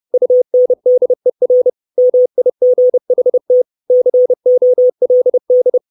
Morse code, heard as a keyed tone, about 30 wpm, 500 Hz. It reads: UNDER MIGHT COLD
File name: 632 Words 30WPM